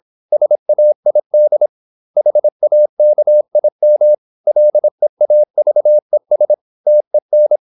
Morse code, heard as a keyed tone, about 26 wpm, 600 Hz. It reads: SAID HAKIM LEAVES TEN
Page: Never 26wpm | CW med Gnister